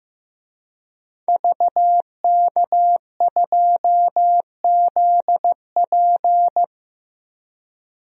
Morse code, heard as a keyed tone, about 15 wpm, 700 Hz. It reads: VK2ZP
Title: DXCC-100 Call Signs - 5 Characters - Repeated 2x - 15wpm